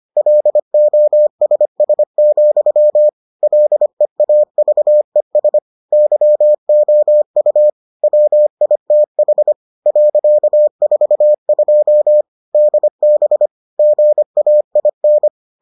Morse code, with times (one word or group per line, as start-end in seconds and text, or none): 0.2-3.1 LOSS,
3.4-5.6 LEAVES
5.9-7.7 YOU
8.0-9.5 WITH
9.9-12.2 .42
12.5-13.5 DB
13.8-15.3 GAIN